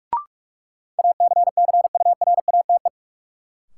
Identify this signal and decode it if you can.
Morse code, heard as a keyed tone, about 44 wpm, 700 Hz. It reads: ACCURATE
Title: Sentences from Top 1200 Words - New Word Drill Alt - 45wpm